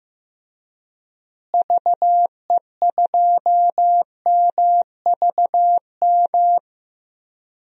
Morse code, heard as a keyed tone, about 15 wpm, 700 Hz. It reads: VE2MVM
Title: DXCC-100 Call Signs - 6 Characters - Repeated 2x - 15wpm